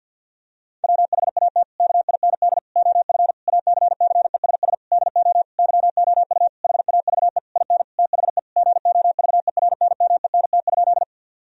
Morse code, heard as a keyed tone, about 50 wpm, 700 Hz. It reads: WHAT KIND OF ACCESS DO YOU HAVE IN THE GOVERNMENT?
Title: Sentences from Top 1000 Words - Review Part 4 - 50wpm